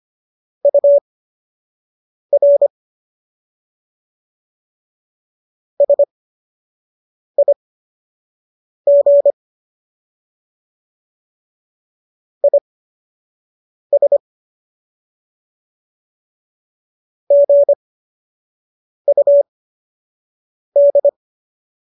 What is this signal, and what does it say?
Morse code, heard as a keyed tone, about 25 wpm, spaced as for 6 wpm, 570 Hz. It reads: UR SIG IS GUD